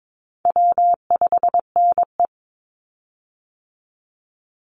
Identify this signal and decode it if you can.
Morse code, heard as a keyed tone, about 22 wpm, 700 Hz. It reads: W5NE
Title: Easy Call Signs - 22wpm